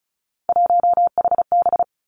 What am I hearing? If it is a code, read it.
Morse code, heard as a keyed tone, about 36 wpm, 700 Hz. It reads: 1HB
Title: Three Challenging Letter-Number Combinations 35wpm